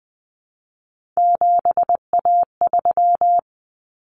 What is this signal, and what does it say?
Morse code, heard as a keyed tone, about 20 wpm, 700 Hz. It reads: 7A3
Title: Common Cut Numbers - 3-Digit 20wpm